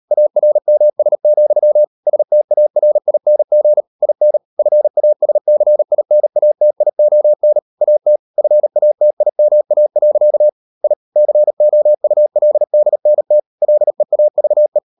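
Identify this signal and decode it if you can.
Morse code, heard as a keyed tone, about 37 wpm, 600 Hz. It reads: ARMS, STARING IN FASCINATION AT FATIMA. I COULDNT LEAVE
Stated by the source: Morse code practice at 38 WPM, sending Ken Follett's novel Never in English, from CW med Gnister.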